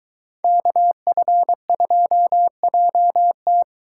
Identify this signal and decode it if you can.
Morse code, heard as a keyed tone, about 23 wpm, 700 Hz. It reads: KF2JT